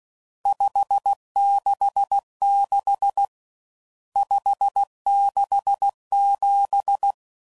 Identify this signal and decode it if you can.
Morse code, heard as a keyed tone, about 16 wpm, 800 Hz. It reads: 566 567
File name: CW Code Groups @15WPM / RSTs / Various RST Nos